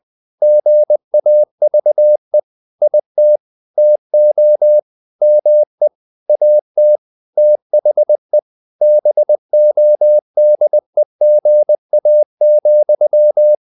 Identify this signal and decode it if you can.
Morse code, heard as a keyed tone, about 20 wpm, 600 Hz. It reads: GAVE IT TO ME AT THE BODEGA,
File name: the old man and the sea / the old man and the sea part 1 20wpm